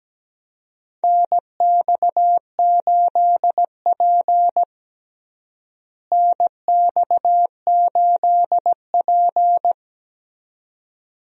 Morse code, heard as a keyed tone, about 17 wpm, 700 Hz. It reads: NX8P NX8P
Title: USA Call Signs - 4 Characters - Repeated 2x - 17wpm